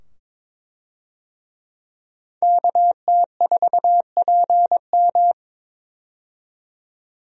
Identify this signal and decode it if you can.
Morse code, heard as a keyed tone, about 22 wpm, 700 Hz. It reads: KT4PM
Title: USA Call Signs - 5 Characters - 22wpm